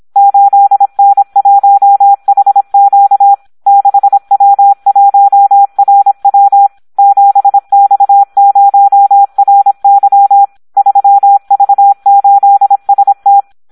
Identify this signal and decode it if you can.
Morse code, heard as a keyed tone, about 26 wpm, 800 Hz, with radio noise. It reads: 8N1HQ 6W1RW 7X0RY 3V8ST
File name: Audio practica CW / SD 1 26